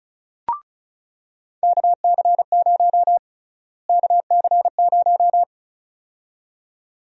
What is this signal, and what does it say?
Morse code, heard as a keyed tone, about 34 wpm, 700 Hz. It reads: KC0 KC0
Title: USA Call Sign Prefixes - Repeated 2x - 35wpm